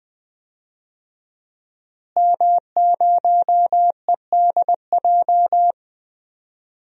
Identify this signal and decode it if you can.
Morse code, heard as a keyed tone, about 20 wpm, 700 Hz. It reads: M0EDJ
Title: DXCC-100 Call Signs - 5 Characters - Repeated 3x - 20wpm